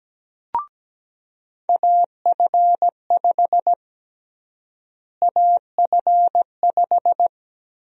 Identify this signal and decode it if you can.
Morse code, heard as a keyed tone, about 17 wpm, 700 Hz. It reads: AF5 AF5
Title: USA Call Sign Prefixes - Repeated 2x - 17wpm